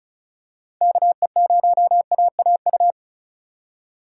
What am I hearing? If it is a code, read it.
Morse code, heard as a keyed tone, about 36 wpm, 700 Hz. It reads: KE0AAU
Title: USA Call Signs - 3x Character Spacing - 35wpm